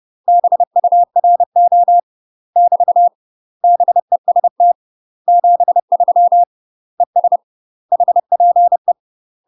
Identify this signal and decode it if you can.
Morse code, heard as a keyed tone, about 30 wpm, 700 Hz. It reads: BURO <BT> BEST 73 ES HPE
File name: UK MASTER QSOs TRAINING / 30WPM / UK MASTER QSOs 0045